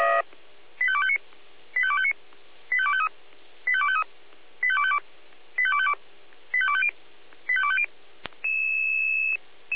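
Five-tone selective call (ZVEI), heard as quick runs of five tones, tones of about 70 ms. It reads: 85259 85259 85253 85253 85252 85252 85250 85250
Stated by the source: ZVEI five-tone selective call received on SDR.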